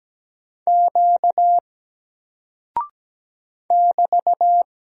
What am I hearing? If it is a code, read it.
Morse code, heard as a keyed tone, about 17 wpm, 700 Hz. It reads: Q <BT>
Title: Single Character (Letter:Number:Punctuation:Prosign) 17wpm